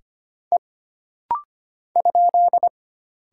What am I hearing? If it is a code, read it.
Morse code, heard as a keyed tone, about 25 wpm, 700 Hz. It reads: E ?
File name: Single Character (Letter:Number:Punctuation:Prosign) - Rapid-Fire - 25wpm